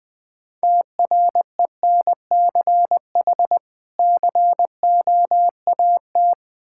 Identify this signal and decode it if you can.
Morse code, heard as a keyed tone, about 20 wpm, 700 Hz. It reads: TRENCH COAT
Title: Difficult Words with Hints - 20wpm